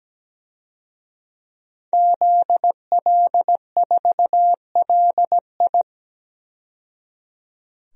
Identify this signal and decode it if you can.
Morse code, heard as a keyed tone, about 17 wpm, 700 Hz. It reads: ZL4LI
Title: DXCC-100 Call Signs - 5 Characters - 17wpm